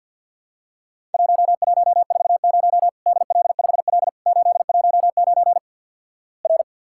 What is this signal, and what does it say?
Morse code, heard as a keyed tone, about 53 wpm, 700 Hz. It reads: 1140 DL5L 819 R
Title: Worked All Europe DX Contest - QTC Exchange - 50wpm